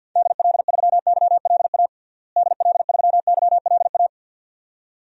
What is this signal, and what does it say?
Morse code, heard as a keyed tone, about 51 wpm, 700 Hz. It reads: DL3YLA DL3YLA
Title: DXCC-100 Call Signs - 6 Characters - Repeated 2x - 50wpm